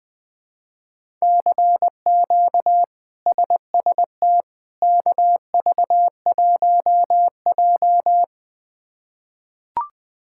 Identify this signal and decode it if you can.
Morse code, heard as a keyed tone, about 20 wpm, 700 Hz. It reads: CQ SST KV1J
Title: K1USN SST QSO Practice - 20wpm